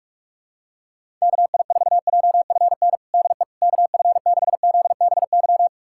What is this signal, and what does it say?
Morse code, heard as a keyed tone, about 47 wpm, 700 Hz. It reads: KI4JFN DE KF6ZBY
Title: USA Callsign DE Callsign - 6 Characters - 45wpm